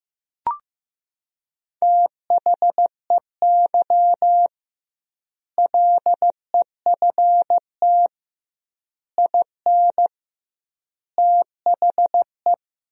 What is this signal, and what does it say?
Morse code, heard as a keyed tone, about 15 wpm, 700 Hz. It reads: THEY LEFT IN THE
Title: 2x Word-Spacing - Sentences from Top 400 Words - 15wpm